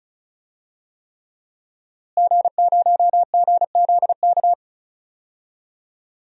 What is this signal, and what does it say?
Morse code, heard as a keyed tone, about 36 wpm, 700 Hz. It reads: G0GZK
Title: DXCC-100 Call Signs - 5 Characters - Repeated 3x - 35wpm